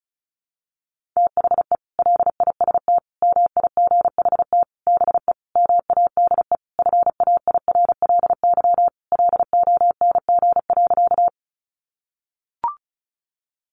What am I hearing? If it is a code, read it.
Morse code, heard as a keyed tone, about 34 wpm, 700 Hz. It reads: THE LIST MIGHT BE MADE FAIRLY LONG.
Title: Sentences from Top 1100 Words 35wpm